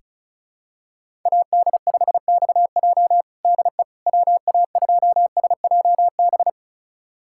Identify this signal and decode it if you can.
Morse code, heard as a keyed tone, about 36 wpm, 700 Hz. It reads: AD5XJ DE WA2SJB
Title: USA Callsign DE Callsign 35wpm